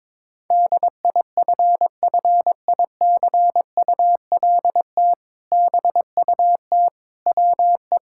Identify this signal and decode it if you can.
Morse code, heard as a keyed tone, about 22 wpm, 700 Hz. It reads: DIFFICULT BUT WE
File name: practice / 22wpm / Sentences with 2-Word Phrases - 22wpm